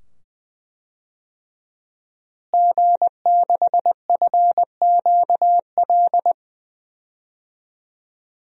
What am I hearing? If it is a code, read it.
Morse code, heard as a keyed tone, about 20 wpm, 700 Hz. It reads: G6FQL
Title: DXCC-100 Call Signs - 5 Characters - Repeated 2x - 20wpm